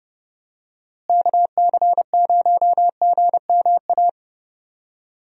Morse code, heard as a keyed tone, about 30 wpm, 700 Hz. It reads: KC0GMA
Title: USA Call Signs - 6 Characters - Repeated 2x - 30wpm